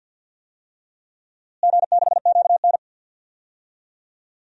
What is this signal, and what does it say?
Morse code, heard as a keyed tone, about 50 wpm, 700 Hz. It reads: G6QN
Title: DXCC-100 Call Signs - 4 Characters - Repeated 3x - 50wpm